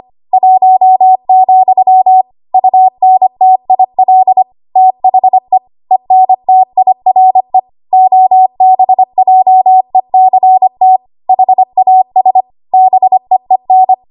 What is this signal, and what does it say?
Morse code, heard as a keyed tone, about 25 wpm, 750 Hz. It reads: 1, UNTIL THE ENTIRE OBJECT HAS BEEN
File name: Archive / 25 WPM / 25WPM